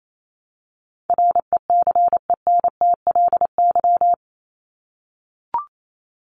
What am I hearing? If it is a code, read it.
Morse code, heard as a keyed tone, about 28 wpm, 700 Hz. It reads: RECENTLY
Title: Top 1001-1250 Words 28wpm